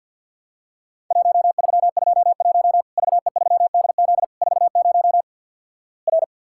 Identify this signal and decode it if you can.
Morse code, heard as a keyed tone, about 50 wpm, 700 Hz. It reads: 1321 VE3DZ 40 R
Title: Worked All Europe DX Contest - QTC Exchange - 50wpm